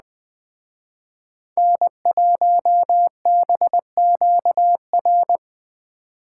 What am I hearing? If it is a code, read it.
Morse code, heard as a keyed tone, about 20 wpm, 700 Hz. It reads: N1BQR